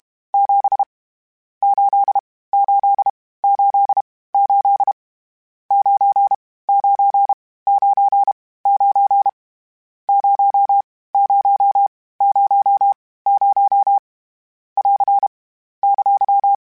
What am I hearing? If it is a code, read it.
Morse code, heard as a keyed tone, about 33 wpm, 800 Hz. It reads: 7 8 8 8 8 9 9 9 9 0 0 0 0 <AR> !
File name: Misc / Misc-mp3s / Numb @30